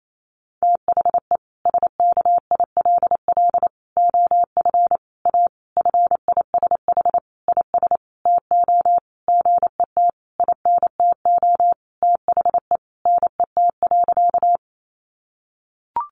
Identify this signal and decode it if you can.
Morse code, heard as a keyed tone, about 28 wpm, 700 Hz. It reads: THE SKILL OF A FISH IS TO GET INTO THE NET.
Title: Sentences from Top 1400 Words 28wpm